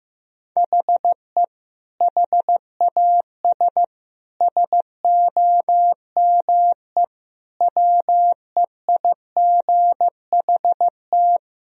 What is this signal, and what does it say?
Morse code, heard as a keyed tone, about 15 wpm, 700 Hz. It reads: HE HAS SOME WEIGHT
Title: Sentences from Top 1000 Words - Review Part 3 - 15wpm